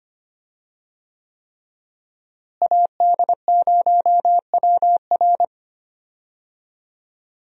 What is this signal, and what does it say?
Morse code, heard as a keyed tone, about 25 wpm, 700 Hz. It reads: AD0WR
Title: USA Call Signs - 5 Characters - Repeated 3x - 25wpm